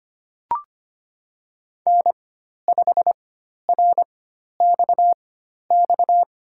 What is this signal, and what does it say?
Morse code, heard as a keyed tone, about 25 wpm, spaced as for 12 wpm, 700 Hz. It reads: N5RXX
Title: USA Call Signs - 4x Character Spacing - 25wpm